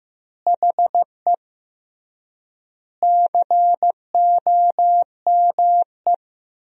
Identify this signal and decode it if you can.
Morse code, heard as a keyed tone, about 16 wpm, 700 Hz. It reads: HE COME
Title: 3x Word-Spacing - Sentences from Top 300 Words - 15wpm